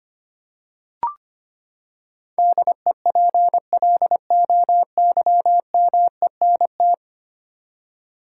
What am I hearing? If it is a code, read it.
Morse code, heard as a keyed tone, about 25 wpm, 700 Hz. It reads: DEPLOYMENT
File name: Ten Letter Words - 25wpm